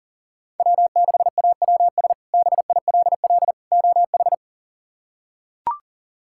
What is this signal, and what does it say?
Morse code, heard as a keyed tone, about 37 wpm, 700 Hz. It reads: W6AWS BILL OH
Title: K1USN SST QSO Practice - 40wpm